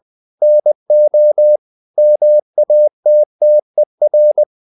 Morse code, heard as a keyed tone, about 20 wpm, 600 Hz. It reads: NO MATTER
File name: the old man and the sea part 2 20wpm